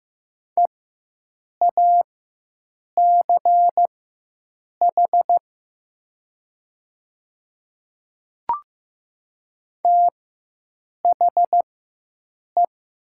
Morse code, heard as a keyed tone, about 15 wpm, spaced as for 7 wpm, 700 Hz. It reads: EACH THE
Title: 4x Character Spacing - Top 100 Words - 15wpm